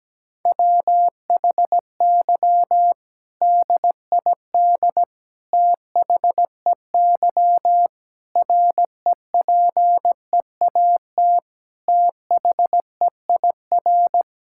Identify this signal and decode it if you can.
Morse code, heard as a keyed tone, about 17 wpm, 700 Hz. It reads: WHY DID THEY REPEAT THEIR
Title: Sentences from Top 1600 Words 17wpm